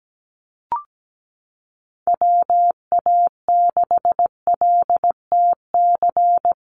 Morse code, heard as a keyed tone, about 17 wpm, 700 Hz. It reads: WA6LTC
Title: USA Call Signs Repeated 3x 17wpm